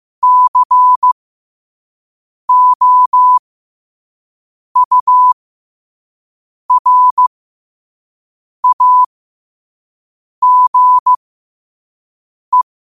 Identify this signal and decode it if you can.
Morse code, heard as a keyed tone, about 15 wpm, spaced as for 5 wpm, 1000 Hz. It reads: COURAGE
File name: qotdfiles / 5wpm